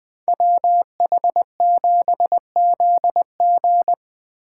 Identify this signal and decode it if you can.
Morse code, heard as a keyed tone, about 24 wpm, 700 Hz. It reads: WH7ZG
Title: Speed Racing - USA 2x2 Call Signs 20wpm